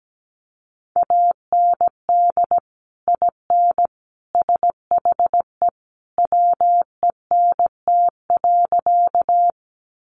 Morse code, heard as a keyed tone, about 17 wpm, 700 Hz. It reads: AND IN SHE WENT.
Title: Sentences from Top 200 Words 17wpm